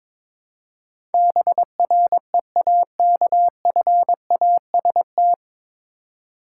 Morse code, heard as a keyed tone, about 22 wpm, 700 Hz. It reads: BREAKFAST
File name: 9 Letter Words - First 3 Letters Given - 22wpm